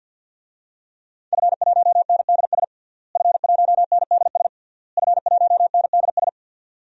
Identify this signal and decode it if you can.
Morse code, heard as a keyed tone, about 49 wpm, 700 Hz. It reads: F1NDS F1NDS F1NDS
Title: DXCC-100 Call Signs - 5 Characters - Repeated 3x - 50wpm